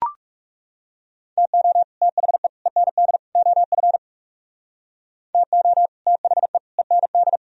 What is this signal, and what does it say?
Morse code, heard as a keyed tone, about 39 wpm, 700 Hz. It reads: TO THE END OF TO THE END